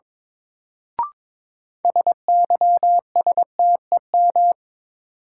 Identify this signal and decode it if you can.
Morse code, heard as a keyed tone, about 22 wpm, 700 Hz. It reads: SYSTEM
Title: Sentences from Top 1000 Words - Review Part 3 - New Word Drill - 22wpm